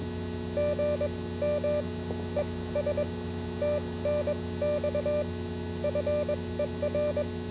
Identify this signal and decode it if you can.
Morse code, heard as a keyed tone, about 22 wpm, spaced as for 17 wpm, 590 Hz, with noise. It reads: GM ES TNX FER